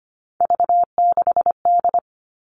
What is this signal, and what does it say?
Morse code, heard as a keyed tone, about 25 wpm, 700 Hz. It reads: V6D
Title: All Cut Numbers - 3-Digit 25wpm